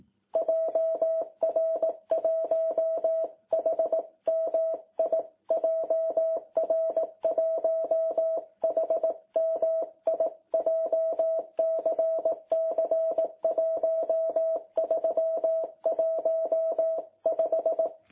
Morse code, heard as a keyed tone, about 18 wpm, 660 Hz. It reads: JR1HMI JR1HMI JCC 1315
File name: jh8cbh / 2021cw